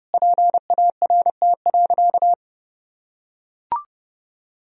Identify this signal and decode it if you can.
Morse code, heard as a keyed tone, about 30 wpm, 700 Hz. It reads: PART.
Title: Sentences from Top 1000 Words - Review Part 1 - 30wpm